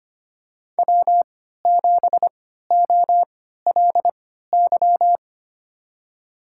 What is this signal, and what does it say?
Morse code, heard as a keyed tone, about 25 wpm, spaced as for 14 wpm, 700 Hz. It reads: W7OLY
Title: USA Call Signs - 3x Character Spacing - 25wpm